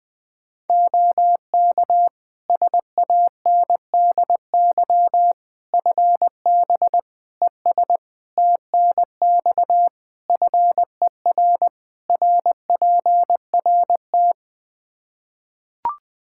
Morse code, heard as a keyed tone, about 20 wpm, 700 Hz. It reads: OK SANDY FB ES TNX FER RPRT
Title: K1USN SST QSO Practice 20wpm